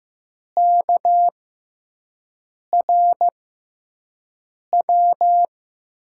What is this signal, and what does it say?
Morse code, heard as a keyed tone, about 15 wpm, spaced as for 5 wpm, 700 Hz. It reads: KRW
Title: USA Call Signs - 6x Character Spacing - 15wpm